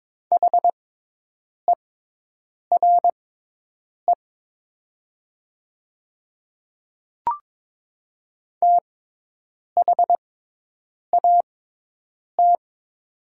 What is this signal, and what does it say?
Morse code, heard as a keyed tone, about 21 wpm, spaced as for 8 wpm, 700 Hz. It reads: HERE THAT
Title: 6x Character Spacing - Top 500 Words - 22wpm